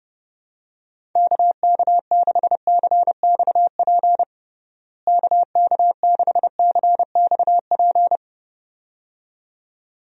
Morse code, heard as a keyed tone, about 32 wpm, 700 Hz. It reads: KK6CXP KK6CXP
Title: USA Call Signs - 6 Characters - Repeated 2x - 30wpm